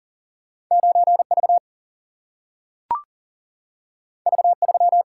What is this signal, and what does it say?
Morse code, heard as a keyed tone, about 40 wpm, 700 Hz. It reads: 9V V3
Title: Single-Letter Single-Number 40wpm